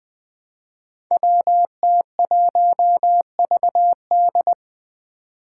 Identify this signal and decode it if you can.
Morse code, heard as a keyed tone, about 20 wpm, 700 Hz. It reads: WT1VD
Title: Five Challenging Letter-Number Combinations 20wpm